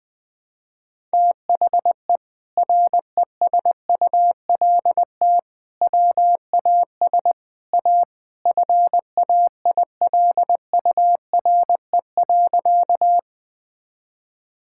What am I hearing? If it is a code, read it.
Morse code, heard as a keyed tone, about 20 wpm, 700 Hz. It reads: THE RESULT WAS A FAILURE.
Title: Sentences from Top 1000 Words - Review Part 6 - 20wpm